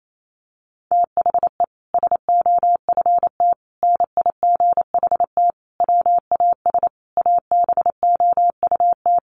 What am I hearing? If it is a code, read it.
Morse code, heard as a keyed tone, about 28 wpm, 700 Hz. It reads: THE SOFT NIGHT WAS ABOUT
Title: Sentences from Top 1500 Words 28wpm